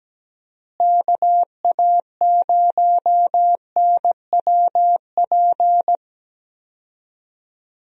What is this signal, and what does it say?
Morse code, heard as a keyed tone, about 17 wpm, 700 Hz. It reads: KA0NWP
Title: USA Call Signs - 6 Characters - Repeated 2x - 17wpm